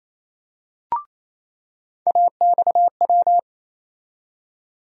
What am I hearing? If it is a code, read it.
Morse code, heard as a keyed tone, about 28 wpm, 700 Hz. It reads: AXW
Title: USA Call Sign Postfixes - 3 Letters 28wpm